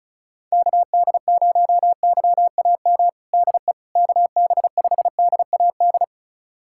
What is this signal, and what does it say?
Morse code, heard as a keyed tone, about 36 wpm, 700 Hz. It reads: KD0YAM DE KB5DAD
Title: USA Callsign DE Callsign - 6 Characters - 35wpm